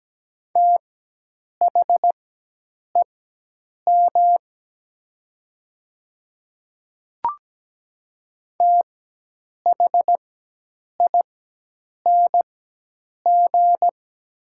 Morse code, heard as a keyed tone, about 17 wpm, spaced as for 8 wpm, 700 Hz. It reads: THEM THING
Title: 4x Character Spacing - Top 100 Words - 17wpm